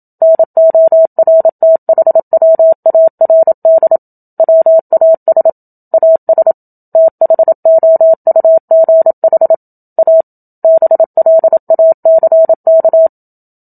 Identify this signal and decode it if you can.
Morse code, heard as a keyed tone, about 27 wpm, 650 Hz. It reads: NORTHWARD WAS AS THOUGH A BLACK
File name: Book Two / 01-Chapter 1 - 27 WPM